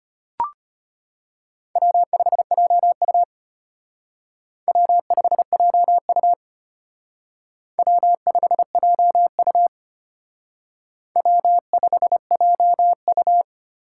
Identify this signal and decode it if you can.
Morse code, heard as a keyed tone, about 32 wpm, 700 Hz. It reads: W5JU W5JU W5JU W5JU